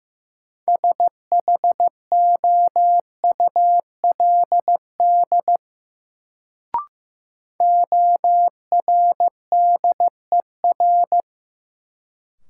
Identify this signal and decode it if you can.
Morse code, heard as a keyed tone, about 15 wpm, 700 Hz. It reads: SHOULD ORDER
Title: Sets of 2 Words - New Word Drill - 15wpm